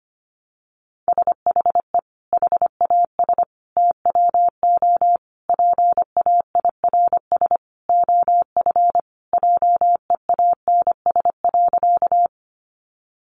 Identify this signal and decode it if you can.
Morse code, heard as a keyed tone, about 25 wpm, 700 Hz. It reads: SHE HAS TWO PAIRS OF JEANS.
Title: Additional Easy Phrases 25wpm